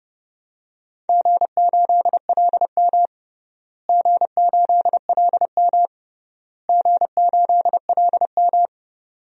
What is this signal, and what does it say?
Morse code, heard as a keyed tone, about 29 wpm, 700 Hz. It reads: G8LM G8LM G8LM
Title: DXCC-100 Call Signs - 4 Characters - Repeated 3x - 30wpm